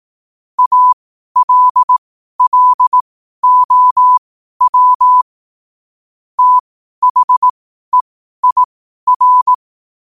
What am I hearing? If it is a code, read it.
Morse code, heard as a keyed tone, about 18 wpm, spaced as for 12 wpm, 1000 Hz. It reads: ALLOW THEIR
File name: 2026-01-07-12wpm